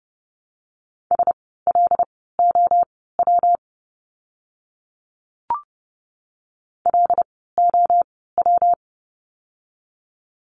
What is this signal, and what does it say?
Morse code, heard as a keyed tone, about 30 wpm, spaced as for 17 wpm, 700 Hz. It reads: SLOW LOW
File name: course / 30:10wpm / 115 Lesson 19f - Letter w - Words 30wpm